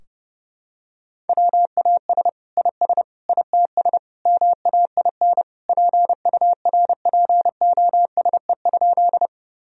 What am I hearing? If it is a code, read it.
Morse code, heard as a keyed tone, about 30 wpm, 700 Hz. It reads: WAS IS ITS MAIN PURPOSE?
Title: Sentences from Top 1000 Words - Review Part 2 - 30wpm